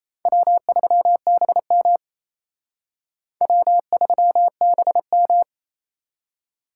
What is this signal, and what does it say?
Morse code, heard as a keyed tone, about 34 wpm, 700 Hz. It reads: W3BM W3BM
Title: Speed Racing - USA 1x2 Call Signs 28wpm